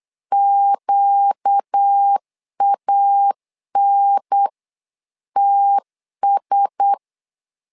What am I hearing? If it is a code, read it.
Morse code, heard as a keyed tone, about 8 wpm, 800 Hz. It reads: QANTS
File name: LEZIONI / 5 LBVC / 5 LBVC 8wpm 2935